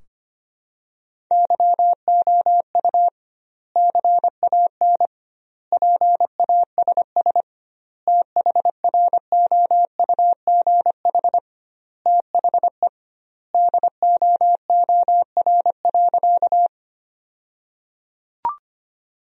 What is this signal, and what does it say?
Morse code, heard as a keyed tone, about 25 wpm, 700 Hz. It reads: YOU CAN PASS THROUGH THE DOOR.
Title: 2x Word-Spacing - Sentences from Top 400 Words - 25wpm